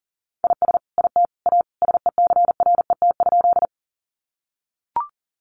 Morse code, heard as a keyed tone, about 40 wpm, 700 Hz. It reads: IS IT A SECRET?
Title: Sentences from Top 1200 Words 40wpm